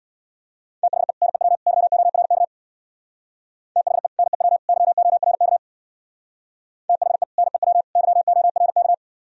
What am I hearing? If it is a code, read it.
Morse code, heard as a keyed tone, about 62 wpm, 700 Hz. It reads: THE NEW YORK THE NEW YORK THE NEW YORK
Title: Speed-Racing - Sets of 3 Words 50wpm